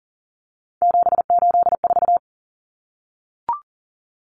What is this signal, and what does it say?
Morse code, heard as a keyed tone, about 40 wpm, 700 Hz. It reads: 784
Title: Numbers - 3-Digit 40wpm